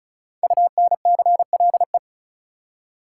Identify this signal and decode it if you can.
Morse code, heard as a keyed tone, about 36 wpm, 700 Hz. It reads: UNCLE
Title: Five Letter Words - 35wpm